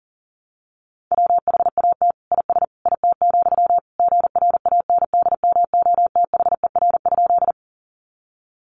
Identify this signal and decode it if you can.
Morse code, heard as a keyed tone, about 40 wpm, 700 Hz. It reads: WHAT IS IT, GRANDMOTHER?
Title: Sentences from Top 1200 Words 40wpm